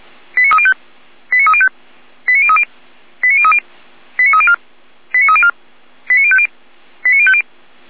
Five-tone selective call (ZVEI), heard as quick runs of five tones, tones of about 70 ms. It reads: 89305 89305 89030 89030 89304 89304 89050 89050
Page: /455/SDR/ZVEI/89020/